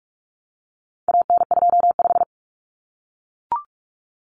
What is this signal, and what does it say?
Morse code, heard as a keyed tone, about 44 wpm, 700 Hz. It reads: AN25
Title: Common Cut Numbers - 4-Digit 45wpm